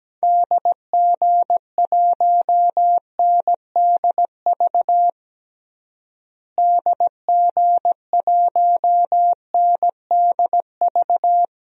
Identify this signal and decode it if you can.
Morse code, heard as a keyed tone, about 17 wpm, 700 Hz. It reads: DG1NDV DG1NDV
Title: DXCC-100 Call Signs - 6 Characters - Repeated 3x - 17wpm